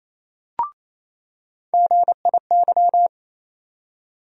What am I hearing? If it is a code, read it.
Morse code, heard as a keyed tone, about 28 wpm, 700 Hz. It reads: GIY
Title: USA Call Sign Postfixes - 3 Letters 28wpm